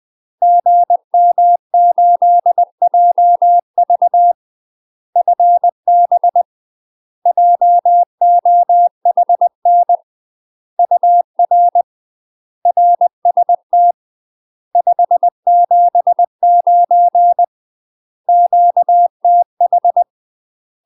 Morse code, heard as a keyed tone, about 20 wpm, 700 Hz. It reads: GM8JV FB JOHN UR RST 579 QTH
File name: UK MASTER QSOs TRAINING / 20WPM / UK MASTER QSOs 0036